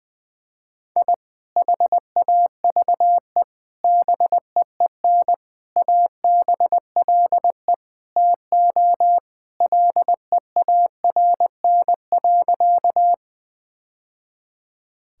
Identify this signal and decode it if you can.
Morse code, heard as a keyed tone, about 20 wpm, 700 Hz. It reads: I HAVE BEEN ABLE TO LEARN.
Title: Sentences with 3-Word Phrases - Spoken Beginning - 20wpm